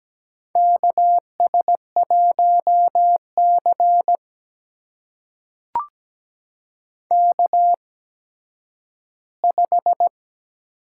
Morse code, heard as a keyed tone, about 17 wpm, spaced as for 10 wpm, 700 Hz. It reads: KS1C K5
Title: USA Call Signs - 4 Characters - 8x Character Spacing - 17wpm